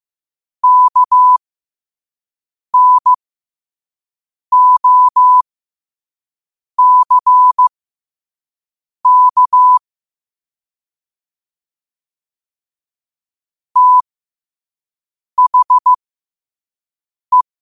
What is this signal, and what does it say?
Morse code, heard as a keyed tone, about 15 wpm, spaced as for 5 wpm, 1000 Hz. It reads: KNOCK THE
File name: qotdfiles / 5wpm